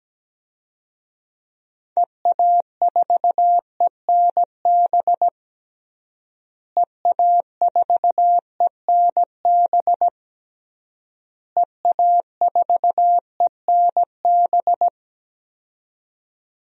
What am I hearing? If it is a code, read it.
Morse code, heard as a keyed tone, about 17 wpm, 700 Hz. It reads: EA4ENB EA4ENB EA4ENB
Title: DXCC-100 Call Signs - 6 Characters - Repeated 3x - 17wpm